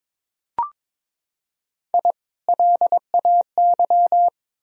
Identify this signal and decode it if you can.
Morse code, heard as a keyed tone, about 22 wpm, 700 Hz. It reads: I LAY
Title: Sentences from Top 1000 Words - Review Part 2 - 22wpm